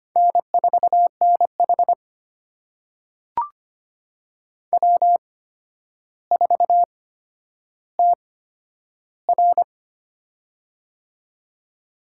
Morse code, heard as a keyed tone, about 25 wpm, spaced as for 11 wpm, 700 Hz. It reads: N4NH W4TR
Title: USA Call Signs - 4 Characters - 8x Character Spacing - 25wpm